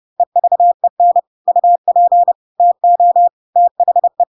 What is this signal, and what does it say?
Morse code, heard as a keyed tone, about 30 wpm, 700 Hz. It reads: EVEN UP TO THE
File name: Pinocchio / Pinocchio ch1 30 30 ik1hge